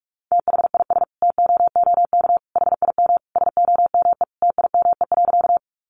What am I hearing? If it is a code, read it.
Morse code, heard as a keyed tone, about 44 wpm, 700 Hz. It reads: THIS TOOK HIM SOME TIME.
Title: Sentences from Top 300 Words 45wpm